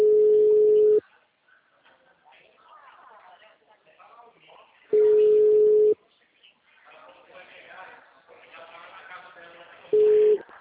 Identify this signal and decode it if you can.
Ringback tone (Europe, 425 Hz), with speech.
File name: out- -2008- - - .92491